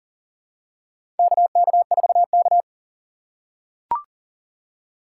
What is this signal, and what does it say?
Morse code, heard as a keyed tone, about 40 wpm, 700 Hz. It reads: KK4K
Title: USA Call Signs - 4 Characters - 2x Character Spacing - 40wpm